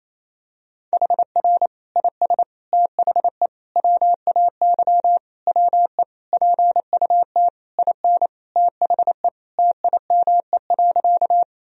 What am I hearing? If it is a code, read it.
Morse code, heard as a keyed tone, about 28 wpm, 700 Hz. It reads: HR IS THE WAY WE PUT IN THE TIME.
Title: Sentences from Top 200 Words and Abbreviations 28wpm